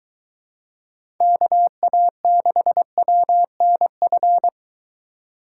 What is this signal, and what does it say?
Morse code, heard as a keyed tone, about 23 wpm, 700 Hz. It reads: KA6WNF